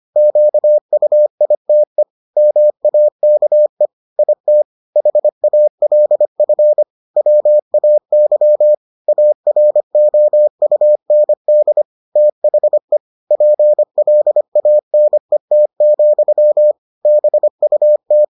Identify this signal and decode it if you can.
Morse code, heard as a keyed tone, about 25 wpm, 600 Hz. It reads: QUITE MAKE IT HALF WAY AROUND THE PLANET, BUT